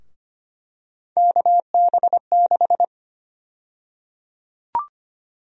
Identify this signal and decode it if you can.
Morse code, heard as a keyed tone, about 25 wpm, 700 Hz. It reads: KB6
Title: USA Call Sign Prefixes - Repeated 2x - 25wpm